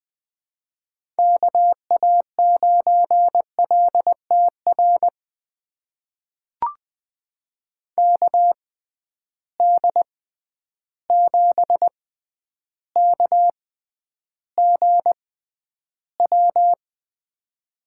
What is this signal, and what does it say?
Morse code, heard as a keyed tone, about 20 wpm, spaced as for 10 wpm, 700 Hz. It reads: KA9LTR KD7KGW
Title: USA Call Signs - 6 Characters - 6x Character Spacing - 20wpm